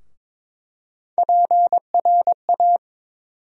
Morse code, heard as a keyed tone, about 22 wpm, 700 Hz. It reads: PRA
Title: USA Call Sign Postfixes - 3 Letters 22wpm